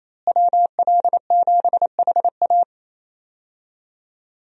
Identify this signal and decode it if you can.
Morse code, heard as a keyed tone, about 28 wpm, 700 Hz. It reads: WL7HA
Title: USA Call Signs - 5 Characters - Repeated 2x - 28wpm